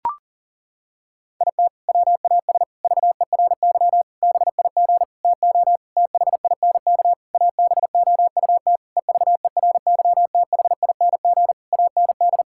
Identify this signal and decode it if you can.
Morse code, heard as a keyed tone, about 40 wpm, 700 Hz. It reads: IT WAS VERY BIG TO THINK ABOUT EVERYTHING AND
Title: Sentences from Top 1600 Words 40wpm